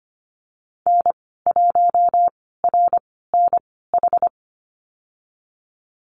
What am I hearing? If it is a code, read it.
Morse code, heard as a keyed tone, about 25 wpm, spaced as for 16 wpm, 700 Hz. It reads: N1RNH
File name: Lesson 15g - Number 5 - Call Signs 25wpm